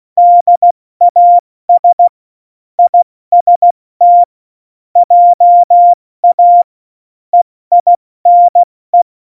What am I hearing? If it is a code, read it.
Morse code, heard as a keyed tone, about 16 wpm, 700 Hz. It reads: DAS IST JA EINE
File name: Morsekurs / Witze / Witz 11. September 2018, 70 F80